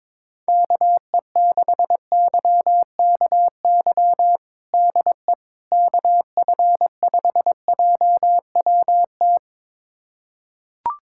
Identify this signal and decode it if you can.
Morse code, heard as a keyed tone, about 21 wpm, 700 Hz. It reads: KE6YKY DE KF5JWT
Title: USA Callsign DE Callsign - 6 Characters - 22wpm